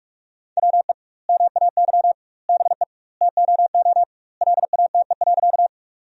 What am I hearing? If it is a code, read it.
Morse code, heard as a keyed tone, about 46 wpm, 700 Hz. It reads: WE MAY BE TOO LATE.
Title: 2x Word-Spacing - Sentences from Top 300 Words - 45wpm